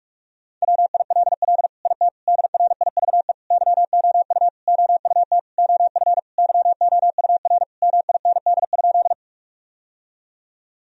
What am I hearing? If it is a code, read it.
Morse code, heard as a keyed tone, about 46 wpm, 700 Hz. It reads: WILL IT DRIVE YOU OUT OF YOUR MIND?
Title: Sentences from Top 1000 Words - Review Part 2 - 45wpm